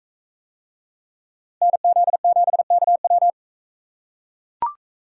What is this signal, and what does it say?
Morse code, heard as a keyed tone, about 42 wpm, 700 Hz. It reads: NZ7KW